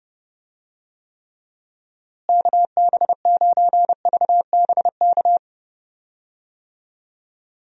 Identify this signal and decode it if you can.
Morse code, heard as a keyed tone, about 30 wpm, 700 Hz. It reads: KB9VBK
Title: USA Call Signs - 6 Characters - Repeated 2x - 30wpm